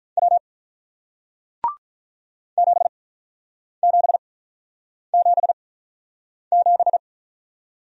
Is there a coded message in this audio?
Yes: Morse code from 0.2 to 7.0 s, heard as a keyed tone, about 47 wpm, spaced as for 9 wpm, 700 Hz. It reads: W 7777